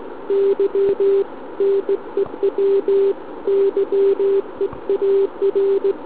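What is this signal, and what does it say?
Morse code, heard as a keyed tone, about 18 wpm, 390 Hz. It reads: Y NEW YEAR
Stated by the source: SM7ALC heard via tropo on 144 MHz with an FT817 and a quarter-wave antenna on a central-heating radiator.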